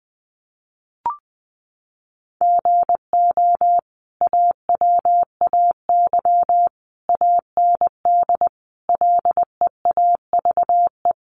Morse code, heard as a keyed tone, about 20 wpm, 700 Hz. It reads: GO AWAY AND LEAVE
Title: Sentences from Top 1300 Words 20wpm